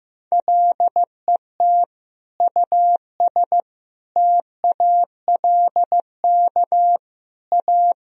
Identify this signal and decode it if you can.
Morse code, heard as a keyed tone, about 16 wpm, 700 Hz. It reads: LET US TALK A
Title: Sentences from Top 400 Words and Abbreviations 15wpm